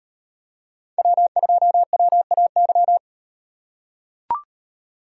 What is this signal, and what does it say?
Morse code, heard as a keyed tone, about 38 wpm, 700 Hz. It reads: W2WAY